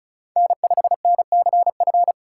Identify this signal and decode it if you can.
Morse code, heard as a keyed tone, about 34 wpm, 700 Hz. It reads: N5NCF
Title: USA Call Signs - 5 Characters - 4x Character Spacing - 35wpm